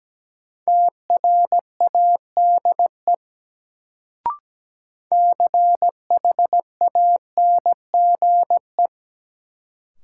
Morse code, heard as a keyed tone, about 17 wpm, 700 Hz. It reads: TRADE CHANGE
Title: Sentences from Top 600 Words - New Word Drill - 17wpm